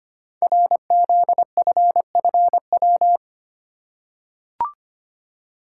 Morse code, heard as a keyed tone, about 25 wpm, 700 Hz. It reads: RZFFW
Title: Code Groups - 5 Letters - 25wpm